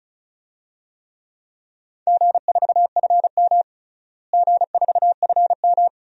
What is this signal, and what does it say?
Morse code, heard as a keyed tone, about 36 wpm, 700 Hz. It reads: G4FM G4FM
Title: DXCC-100 Call Signs - 4 Characters - Repeated 2x - 35wpm